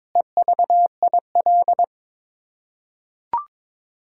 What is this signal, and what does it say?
Morse code, heard as a keyed tone, about 22 wpm, 700 Hz. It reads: EVIL
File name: Sets of 9 Words - 22wpm